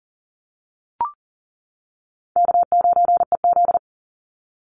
Morse code, heard as a keyed tone, about 40 wpm, 700 Hz. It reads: K9EZ
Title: NA SOTA Chaser Callsigns 40wpm